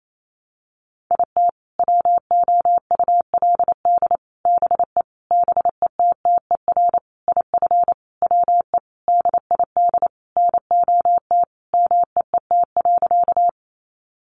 Morse code, heard as a keyed tone, about 28 wpm, 700 Hz. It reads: IT WOULD BE BETTER IF WE DID NOT MEET.
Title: Sentences from Top 1100 Words 28wpm